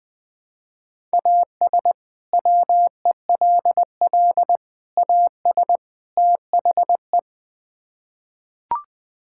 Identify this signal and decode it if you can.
Morse code, heard as a keyed tone, about 20 wpm, 700 Hz. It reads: AS WELL AS THE
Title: Build A Phrase - 4-Word Phrases - 20wpm